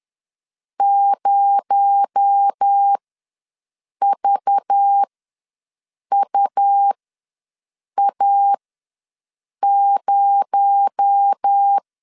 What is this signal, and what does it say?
Morse code, heard as a keyed tone, about 10 wpm, spaced as for 6 wpm, 800 Hz. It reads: 0VUA0
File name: LEZIONI / 6 PKX0 / 6 PKX0 ws 10wpm 4220